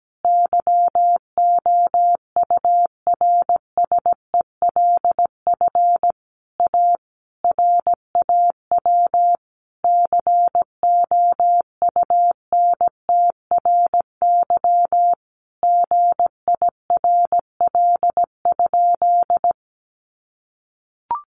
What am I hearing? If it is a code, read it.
Morse code, heard as a keyed tone, about 17 wpm, 700 Hz. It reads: YOURSELF A RAW COUNTRY GIRL?
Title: Sentences from Top 1500 Words 17wpm